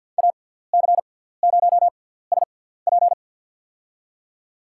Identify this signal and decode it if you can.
Morse code, heard as a keyed tone, about 50 wpm, spaced as for 17 wpm, 700 Hz. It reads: AC0SP
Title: USA Call Signs - 6x Character Spacing - 50wpm